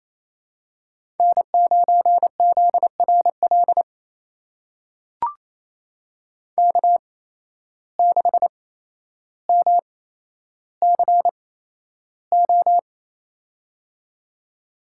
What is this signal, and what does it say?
Morse code, heard as a keyed tone, about 28 wpm, spaced as for 12 wpm, 700 Hz. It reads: N9ZRL K6MCO
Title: USA Call Signs - 5 Characters - 8x Character Spacing - 28wpm